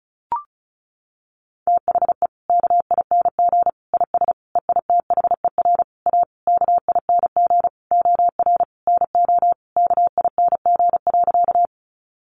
Morse code, heard as a keyed tone, about 34 wpm, 700 Hz. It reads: THE KING IS EITHER A KING OR NO KING.
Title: Sentences from Top 600 Words 35wpm